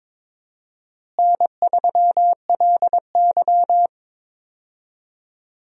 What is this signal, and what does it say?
Morse code, heard as a keyed tone, about 22 wpm, 700 Hz. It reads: N3LY
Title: USA Call Signs - 4 Characters - Repeated 2x - 22wpm